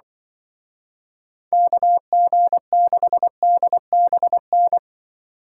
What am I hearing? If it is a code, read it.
Morse code, heard as a keyed tone, about 24 wpm, 700 Hz. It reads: KG6DBN
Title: Speed Racing - USA Call Signs 20wpm